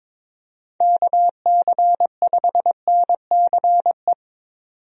Morse code, heard as a keyed tone, about 22 wpm, 700 Hz. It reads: KC5NCE
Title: USA Call Signs - 6 Characters - 22wpm